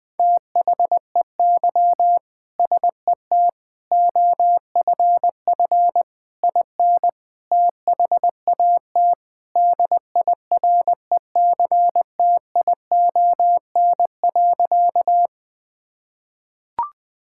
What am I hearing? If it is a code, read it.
Morse code, heard as a keyed tone, about 21 wpm, 700 Hz. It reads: THEY SET OFF IN THAT DIRECTION.
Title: Sentences from Top 1000 Words - Review Part 5 - 20wpm